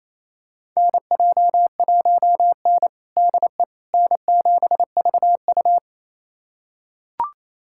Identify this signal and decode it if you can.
Morse code, heard as a keyed tone, about 28 wpm, 700 Hz. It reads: NJ1N DE N7VU
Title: USA Callsign DE Callsign - 4 Characters - 28wpm